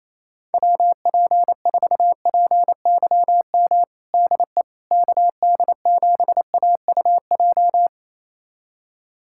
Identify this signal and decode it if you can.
Morse code, heard as a keyed tone, about 28 wpm, 700 Hz. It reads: WP4PYM DE KD7AUJ
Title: USA Callsign DE Callsign - 6 Characters - 28wpm